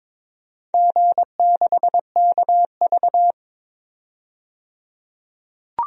Morse code, heard as a keyed tone, about 22 wpm, 700 Hz. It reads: G6KV
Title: DXCC-100 Call Signs - 4 Characters - 22wpm